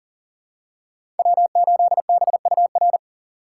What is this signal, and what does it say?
Morse code, heard as a keyed tone, about 40 wpm, 700 Hz. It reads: W8BUR
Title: USA Call Signs - 6x Character Spacing - 40wpm